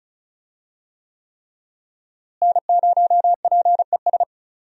Morse code, heard as a keyed tone, about 34 wpm, 700 Hz. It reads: N0PES
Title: USA Call Signs - 5 Characters - Repeated 3x - 35wpm